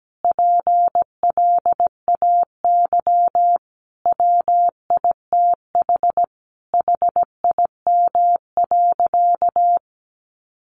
Morse code, heard as a keyed tone, about 17 wpm, 700 Hz. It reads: PLAY WITH HIM.
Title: Sentences from Top 1500 Words 17wpm